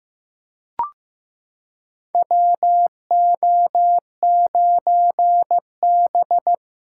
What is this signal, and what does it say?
Morse code, heard as a keyed tone, about 15 wpm, 700 Hz. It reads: WO9B
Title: USA Callsign DE Callsign - 4 Characters - 15wpm